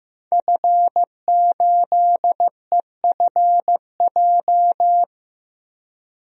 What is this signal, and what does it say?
Morse code, heard as a keyed tone, about 15 wpm, 700 Hz. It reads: F8EFJ
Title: DXCC-100 Call Signs - 5 Characters - 15wpm